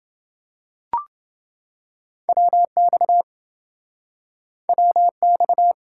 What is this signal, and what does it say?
Morse code, heard as a keyed tone, about 28 wpm, 700 Hz. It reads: WX WX